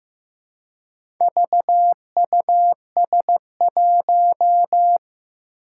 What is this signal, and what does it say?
Morse code, heard as a keyed tone, about 15 wpm, 700 Hz. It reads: VUS1
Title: Four Challenging Letter-Number Combinations 15wpm